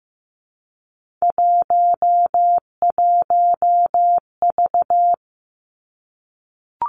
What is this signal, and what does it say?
Morse code, heard as a keyed tone, about 15 wpm, 700 Hz. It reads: 11V
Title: Three Challenging Letter-Number Combinations 15wpm